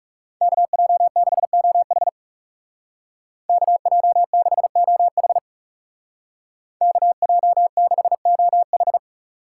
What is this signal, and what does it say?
Morse code, heard as a keyed tone, about 39 wpm, 700 Hz. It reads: KJ6OH KJ6OH KJ6OH